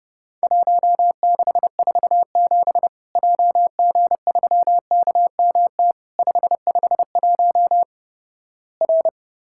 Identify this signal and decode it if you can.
Morse code, heard as a keyed tone, about 32 wpm, 700 Hz. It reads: 1647 JG3KMT 551 R
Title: Worked All Europe DX Contest - QTC Exchange - 30wpm